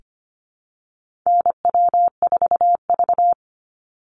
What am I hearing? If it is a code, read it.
Morse code, heard as a keyed tone, about 25 wpm, 700 Hz. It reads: NW4V
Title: Call Signs Repeated 3x - 25wpm